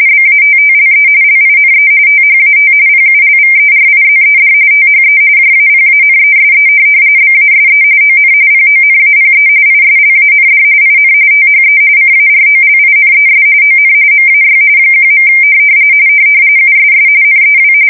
RTTY, about 45.45 baud. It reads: BATTLE LINE THAT PULVERIZED THE JAPANESE BASE ON PONAPE ISLAND IN THE CAROLINES ON MAY 1. USS NEW
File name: fsk170 / USS-New-Jersey